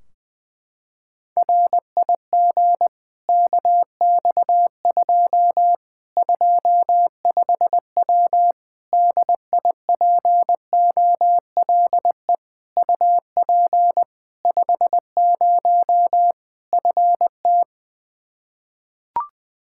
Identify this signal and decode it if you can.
Morse code, heard as a keyed tone, about 20 wpm, 700 Hz. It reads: RIG KX2 25W DIPOLE UP 50 FT
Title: Practice QSO - Exchange 2 - Info - 20wpm